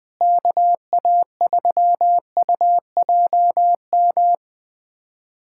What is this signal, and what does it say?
Morse code, heard as a keyed tone, about 21 wpm, 700 Hz. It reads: KA3UJM
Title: USA Callsign DE Callsign - 6 Characters - 20wpm